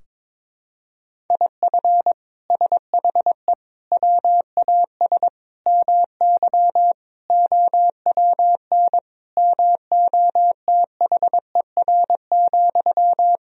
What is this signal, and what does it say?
Morse code, heard as a keyed tone, about 22 wpm, 700 Hz. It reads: IF SHE WAS MY OWN MOTHER,
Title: Sentences from Top 1000 Words - Review Part 2 - 22wpm